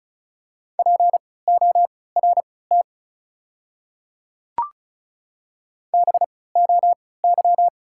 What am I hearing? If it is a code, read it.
Morse code, heard as a keyed tone, about 36 wpm, spaced as for 20 wpm, 690 Hz. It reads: PORT BOY
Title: 3x Character Spacing - Top 300 Words - 35wpm